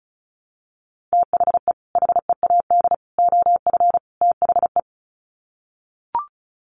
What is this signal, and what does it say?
Morse code, heard as a keyed tone, about 34 wpm, 700 Hz. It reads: THE HEAD OF THE
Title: Sets of 4 Words 35wpm